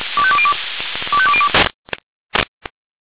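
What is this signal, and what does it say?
Five-tone selective call (ZVEI), heard as quick runs of five tones, tones of about 70 ms. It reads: 25202
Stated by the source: Five-tone ZVEI selective call received by SDR.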